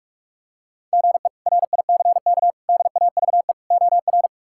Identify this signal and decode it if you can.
Morse code, heard as a keyed tone, about 41 wpm, 700 Hz. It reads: GE RICK DAVE OR
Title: K1USN SST QSO Practice - 45wpm